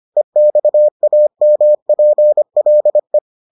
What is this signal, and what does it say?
Morse code, heard as a keyed tone, about 25 wpm, 600 Hz. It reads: EXAMPLE